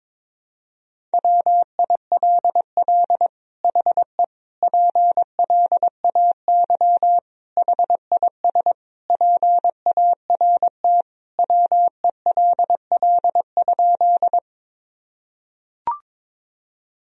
Morse code, heard as a keyed tone, about 21 wpm, 700 Hz. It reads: WILL HE PLAY HIS PART WELL?
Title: Sentences from Top 1000 Words - Review Part 1 - 22wpm